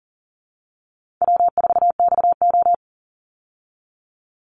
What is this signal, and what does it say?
Morse code, heard as a keyed tone, about 40 wpm, 700 Hz. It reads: W4XO
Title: Easy Call Signs Repeated 3x - 40wpm